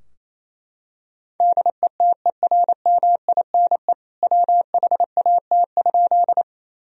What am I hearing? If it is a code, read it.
Morse code, heard as a keyed tone, about 28 wpm, 700 Hz. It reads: DETERMINE WHAT?
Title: Sentences from Top 1000 Words - Review Part 4 - 28wpm